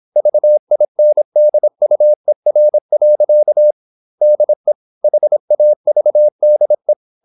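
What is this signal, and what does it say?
Morse code, heard as a keyed tone, about 26 wpm, 600 Hz. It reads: VINDUER. DE HAVDE
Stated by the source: High-speed (QRQ) Morse at 26 wpm.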